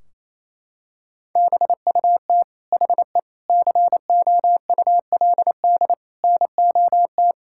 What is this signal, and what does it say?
Morse code, heard as a keyed tone, about 28 wpm, 700 Hz. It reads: BUT HE COULD NOT
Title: Sentences from Top 1600 Words 28wpm